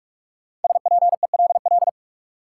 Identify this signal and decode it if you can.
Morse code, heard as a keyed tone, about 44 wpm, 700 Hz. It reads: SPELL
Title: Sentences from Top 1000 Words - Review Part 1 - New Word Drill - 45wpm